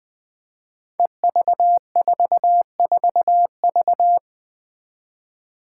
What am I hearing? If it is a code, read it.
Morse code, heard as a keyed tone, about 20 wpm, 700 Hz. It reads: EV44V
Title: Five Challenging Letter-Number Combinations 20wpm